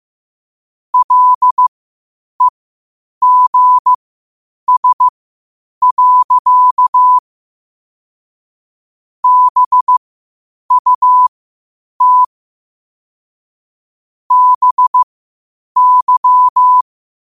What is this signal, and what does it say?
Morse code, heard as a keyed tone, about 15 wpm, spaced as for 8 wpm, 1000 Hz. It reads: LEGS. BUT BY
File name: qotdfiles / 8wpm